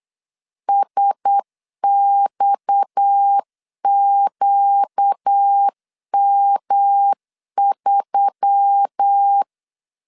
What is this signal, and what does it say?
Morse code, heard as a keyed tone, about 8 wpm, 800 Hz. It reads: SXQM3
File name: LEZIONI / 7 GF9Z / 7 GF 8wpm 3108